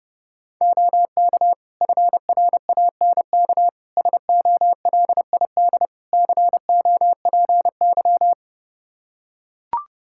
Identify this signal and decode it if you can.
Morse code, heard as a keyed tone, about 30 wpm, 700 Hz. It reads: OK FRANK SOLID COPY
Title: Practice QSO - Exchange 3 - More Info - 30wpm